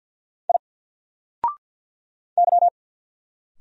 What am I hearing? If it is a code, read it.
Morse code, heard as a keyed tone, about 50 wpm, 700 Hz. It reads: I Y
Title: Single Character (Letter:Number:Punctuation:Prosign) 50wpm